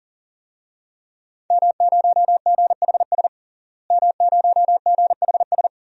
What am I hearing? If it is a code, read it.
Morse code, heard as a keyed tone, about 39 wpm, 700 Hz. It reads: M0GHS M0GHS
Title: DXCC-100 Call Signs - 5 Characters - Repeated 2x - 40wpm